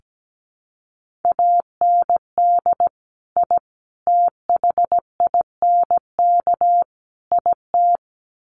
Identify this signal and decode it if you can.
Morse code, heard as a keyed tone, about 17 wpm, 700 Hz. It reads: AND I THINK IT
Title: Sets of 4 Words 17wpm